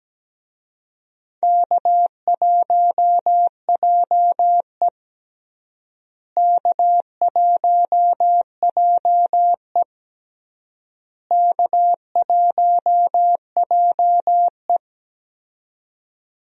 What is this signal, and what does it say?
Morse code, heard as a keyed tone, about 17 wpm, 700 Hz. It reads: K1JE K1JE K1JE
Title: USA Call Signs - 4 Characters - Repeated 3x - 17wpm